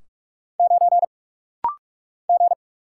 Morse code, heard as a keyed tone, about 46 wpm, 700 Hz. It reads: 9 G
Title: Single Letter-Number Alt - Rapid-Fire 45wpm